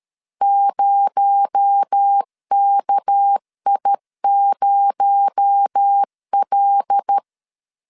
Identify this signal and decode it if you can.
Morse code, heard as a keyed tone, about 13 wpm, 800 Hz. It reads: 0KI0L
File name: LEZIONI / 6 PKX0 / 6 PKX0 12wpm 2231